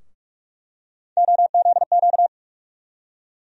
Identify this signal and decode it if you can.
Morse code, heard as a keyed tone, about 48 wpm, 700 Hz. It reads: OZQ
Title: USA Call Sign Postfixes - 3 Letters 45wpm